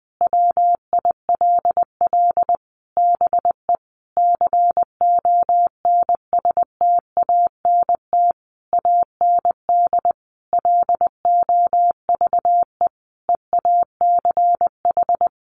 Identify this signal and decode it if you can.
Morse code, heard as a keyed tone, about 20 wpm, 700 Hz. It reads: WILL BE CONSTANT AND LOVE EACH
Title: Sentences from Top 1500 Words 20wpm